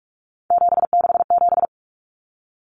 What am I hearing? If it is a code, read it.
Morse code, heard as a keyed tone, about 44 wpm, 700 Hz. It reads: 767
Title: Numbers - 3-Digit 45wpm